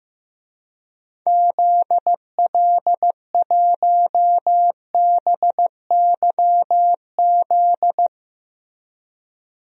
Morse code, heard as a keyed tone, about 15 wpm, 700 Hz. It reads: ZL1BYZ
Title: DXCC-100 Call Signs - 6 Characters - Repeated 2x - 15wpm